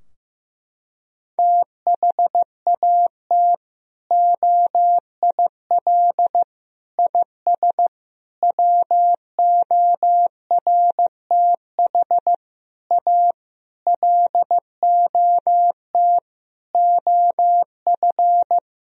Morse code, heard as a keyed tone, about 15 wpm, 700 Hz. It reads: THAT OIL IS WORTH A LOT OF
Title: Sentences from Top 1000 Words - Review Part 4 - 15wpm